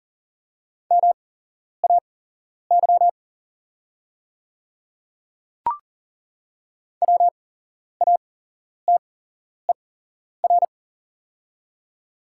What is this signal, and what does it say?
Morse code, heard as a keyed tone, about 40 wpm, spaced as for 11 wpm, 700 Hz. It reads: MAY WATER
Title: 8x Character Spacing - Top 100 Words - 40wpm